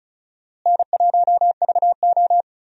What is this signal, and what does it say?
Morse code, heard as a keyed tone, about 36 wpm, 700 Hz. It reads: N1VO
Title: USA Call Signs - 4 Characters - Repeated 2x - 35wpm